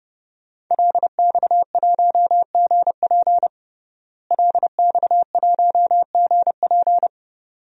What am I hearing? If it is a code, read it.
Morse code, heard as a keyed tone, about 30 wpm, 700 Hz. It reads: LX1GP LX1GP
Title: DXCC-100 Call Signs - 5 Characters - Repeated 2x - 30wpm